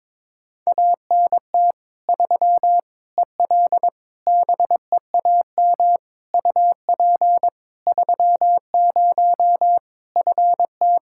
Morse code, heard as a keyed tone, about 22 wpm, 700 Hz. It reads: ANT 3 EL BEAM UP 30 FT
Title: Practice QSO - Exchange 2 - Info - 22wpm